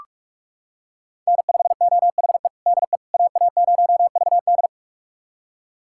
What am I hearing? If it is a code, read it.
Morse code, heard as a keyed tone, about 46 wpm, 700 Hz. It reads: N5OHE DE AA0VD
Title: USA Callsign DE Callsign - 5 Characters - 45wpm